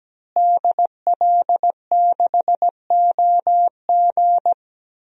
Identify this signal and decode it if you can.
Morse code, heard as a keyed tone, about 17 wpm, 700 Hz. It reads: DL6OG
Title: DXCC-100 Call Signs - 5 Characters - Repeated 2x - 17wpm